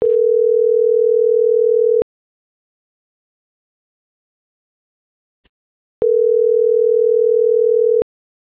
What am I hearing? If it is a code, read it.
Ringback tone (North America), 440 + 480 Hz.